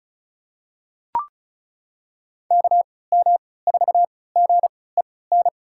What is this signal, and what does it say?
Morse code, heard as a keyed tone, about 36 wpm, spaced as for 20 wpm, 700 Hz. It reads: KM4GEN
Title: USA Call Signs - 3x Character Spacing - 35wpm